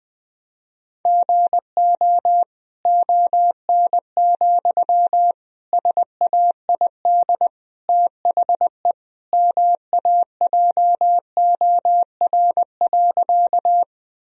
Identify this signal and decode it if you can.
Morse code, heard as a keyed tone, about 20 wpm, 700 Hz. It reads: GO ON, SAID THE MAJOR.
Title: Sentences from Top 1000 Words - Review Part 4 - 20wpm